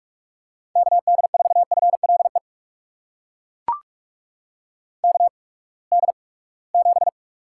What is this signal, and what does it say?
Morse code, heard as a keyed tone, about 42 wpm, spaced as for 25 wpm, 700 Hz. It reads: KD4FLE KD7
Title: USA Call Signs - 6 Characters - 8x Character Spacing - 45wpm